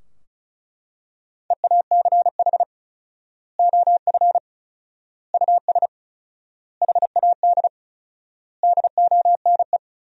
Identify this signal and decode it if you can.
Morse code, heard as a keyed tone, about 34 wpm, 700 Hz. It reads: EACH OF US HAD DONE
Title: 4x Word-Spacing - Sentences from Top 500 Words - 35wpm